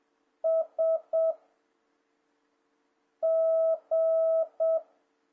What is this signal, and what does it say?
Morse code, heard as a keyed tone, about 7 wpm, 650 Hz. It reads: S G
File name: jh8cbh / 2026cw